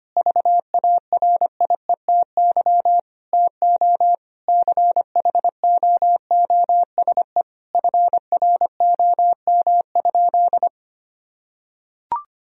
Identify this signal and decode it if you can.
Morse code, heard as a keyed tone, about 25 wpm, 700 Hz. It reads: VARIETY TO CHOOSE FROM?
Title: Sentences from Top 1000 Words - Review Part 4 - 25wpm